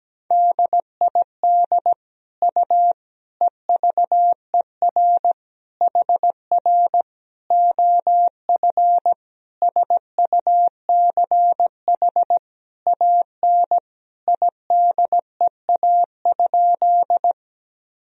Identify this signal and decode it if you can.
Morse code, heard as a keyed tone, about 17 wpm, 700 Hz. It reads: DID U EVER HR OF SUCH AN IDEA?
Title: Sentences from Top 400 Words and Abbreviations 17wpm